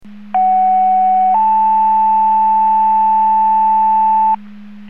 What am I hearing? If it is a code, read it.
Two-tone page, heard as a one-second tone, then a three-second tone, about 750 then 880 Hz. It is heard from 0.3 to 4.4 s.